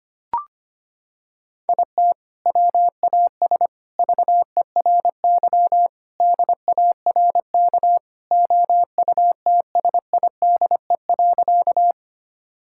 Morse code, heard as a keyed tone, about 25 wpm, 700 Hz. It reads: IT WAS VERY DARK OUTSIDE.
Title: Sentences from Top 1000 Words - Review Part 6 - 25wpm